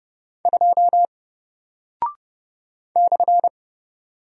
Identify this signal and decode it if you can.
Morse code, heard as a keyed tone, about 30 wpm, 700 Hz. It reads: 2 /
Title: Single Character (Letter:Number:Punctuation:Prosign) Alt 30wpm